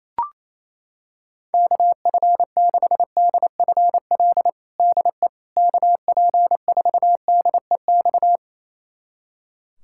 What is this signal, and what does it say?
Morse code, heard as a keyed tone, about 28 wpm, 700 Hz. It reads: KF6DFL DE KP4DEX
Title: USA Callsign DE Callsign - 6 Characters - 28wpm